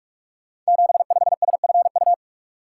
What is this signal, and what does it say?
Morse code, heard as a keyed tone, about 44 wpm, 700 Hz. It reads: 75SFU
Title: Code Groups - 5 Letter:Numbers - 45wpm